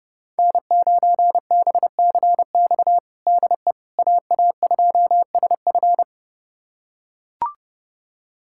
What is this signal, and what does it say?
Morse code, heard as a keyed tone, about 30 wpm, 700 Hz. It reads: N9BCX DE AA2SF
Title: USA Callsign DE Callsign - 5 Characters - 30wpm